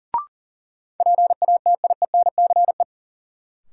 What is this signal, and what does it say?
Morse code, heard as a keyed tone, about 40 wpm, 700 Hz. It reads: PATIENCE
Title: Sentences from Top 1300 Words - New Word Drill - 40wpm